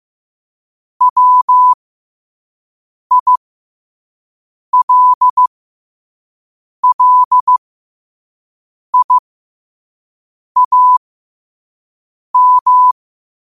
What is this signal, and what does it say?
Morse code, heard as a keyed tone, about 15 wpm, spaced as for 5 wpm, 1000 Hz. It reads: WILLIAM
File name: qotdfiles / 5wpm